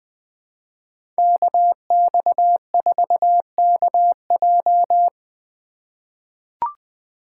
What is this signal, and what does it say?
Morse code, heard as a keyed tone, about 20 wpm, 700 Hz. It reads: KX4KJ
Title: USA Call Signs - 2x Character Spacing - 20wpm